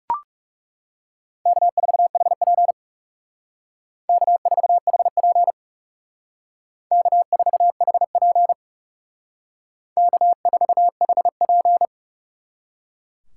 Morse code, heard as a keyed tone, about 36 wpm, 700 Hz. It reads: K4HP K4HP K4HP K4HP
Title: Speed Racing - USA 1x2 Call Signs 30wpm